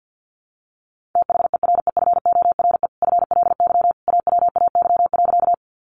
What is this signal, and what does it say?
Morse code, heard as a keyed tone, about 50 wpm, 700 Hz. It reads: THEREFORE FLY AWAY.
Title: Sentences from Top 800 Words 50wpm